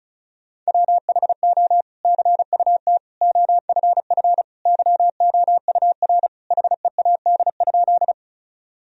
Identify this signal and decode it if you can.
Morse code, heard as a keyed tone, about 36 wpm, 700 Hz. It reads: WHO CUT OFF YOUR HEAD?
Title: Sentences from Top 1000 Words - Review Part 1 - 35wpm